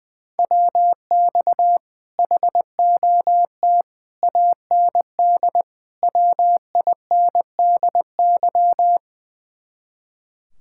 Morse code, heard as a keyed tone, about 20 wpm, 700 Hz. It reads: WX HOT AND WINDY
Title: Speed-Racing Common QSO Phrases - 20wpm